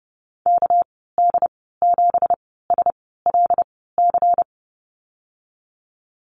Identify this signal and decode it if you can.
Morse code, heard as a keyed tone, about 30 wpm, spaced as for 17 wpm, 700 Hz. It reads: KD7SLC
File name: Lesson 37g - Letter x - Call Signs 30wpm